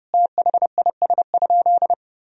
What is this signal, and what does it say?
Morse code, heard as a keyed tone, about 30 wpm, 700 Hz. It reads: THIS?
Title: Sentences from Top 1000 Words - Review Part 1 - 30wpm